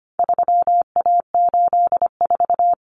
Morse code, heard as a keyed tone, about 25 wpm, 700 Hz. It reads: 3A84
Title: Common Cut Numbers - 4-Digit 25wpm